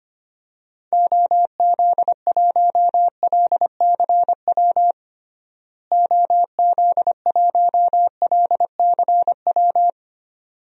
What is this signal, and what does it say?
Morse code, heard as a keyed tone, about 25 wpm, 700 Hz. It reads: OZ1LCW OZ1LCW
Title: DXCC-100 Call Signs - 6 Characters - Repeated 2x - 25wpm